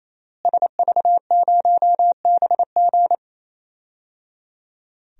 Morse code, heard as a keyed tone, about 28 wpm, 700 Hz. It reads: SV0BG
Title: DXCC-100 Call Signs - 5 Characters - Repeated 3x - 28wpm